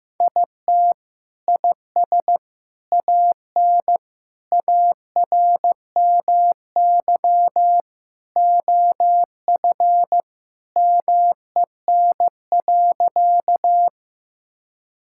Morse code, heard as a keyed tone, about 15 wpm, 700 Hz. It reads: IT IS AN ARMY OF MEN.
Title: Sentences from Top 1000 Words - Review Part 5 - 15wpm